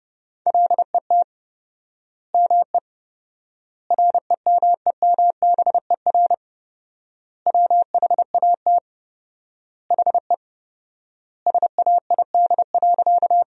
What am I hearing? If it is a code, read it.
Morse code, heard as a keyed tone, about 30 wpm, 700 Hz. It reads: LET ME REMEMBER WHAT HE SAID.
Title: 4x Word-Spacing - Sentences from Top 400 Words - 30wpm